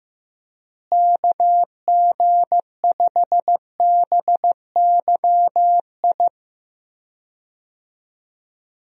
Morse code, heard as a keyed tone, about 15 wpm, 700 Hz. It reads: KG5BYI
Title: USA Call Signs - 6 Characters - Repeated 3x - 15wpm